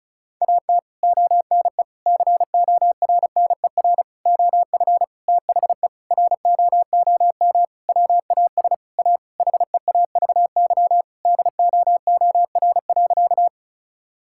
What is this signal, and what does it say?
Morse code, heard as a keyed tone, about 36 wpm, 700 Hz. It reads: AT ONE CORNER OF THE ROOM WAS A HEAVY DOOR.
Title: Sentences from Top 1600 Words 35wpm